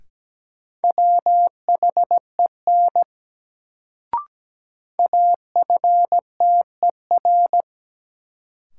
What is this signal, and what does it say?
Morse code, heard as a keyed tone, about 17 wpm, 700 Hz. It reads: WHEN AFTER
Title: Sets of 2 Words - New Word Drill - 17wpm